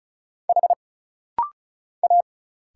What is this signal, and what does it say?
Morse code, heard as a keyed tone, about 34 wpm, 700 Hz. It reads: H A
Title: Single Letter-Number Alt - Rapid-Fire 35wpm